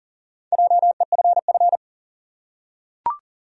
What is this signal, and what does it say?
Morse code, heard as a keyed tone, about 40 wpm, 700 Hz. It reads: JEFF
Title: USA Common Names - Max 5 Letters - 40wpm